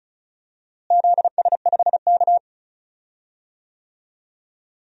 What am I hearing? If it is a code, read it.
Morse code, heard as a keyed tone, about 36 wpm, 700 Hz. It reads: ZS5K
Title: DXCC-100 Call Signs - 4 Characters - Repeated 3x - 35wpm